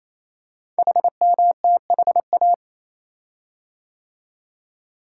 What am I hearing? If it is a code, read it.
Morse code, heard as a keyed tone, about 27 wpm, 700 Hz. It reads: HMTHA
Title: Code Groups - 5 Letters - 28wpm